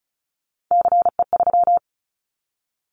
Morse code, heard as a keyed tone, about 34 wpm, 700 Hz. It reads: CE3
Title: Callsign Prefixes for Top 100 DXCC - 35wpm